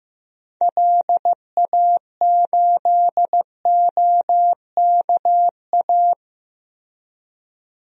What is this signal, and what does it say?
Morse code, heard as a keyed tone, about 15 wpm, 700 Hz. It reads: LA8OKA
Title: DXCC-100 Call Signs - 6 Characters - 15wpm